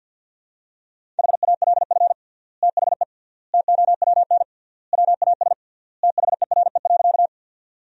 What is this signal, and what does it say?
Morse code, heard as a keyed tone, about 47 wpm, 700 Hz. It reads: HALF THE TOWN WAS THERE.
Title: 3x Word-Spacing - Sentences from Top 500 Words - 50wpm